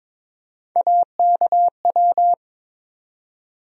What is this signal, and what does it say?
Morse code, heard as a keyed tone, about 22 wpm, 700 Hz. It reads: AKW
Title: USA Call Sign Postfixes - 3 Letters 22wpm